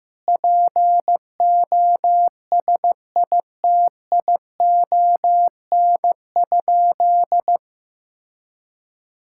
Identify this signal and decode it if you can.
Morse code, heard as a keyed tone, about 15 wpm, 700 Hz. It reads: POSITION?
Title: Sentences from Top 1000 Words - Review Part 5 - 15wpm